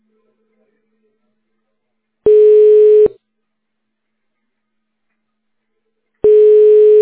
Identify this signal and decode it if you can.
Telephone call progress tone (425 Hz, 1 s on, 3 s off), heard as a telephone network's tone, about 420 Hz. Ringback tone.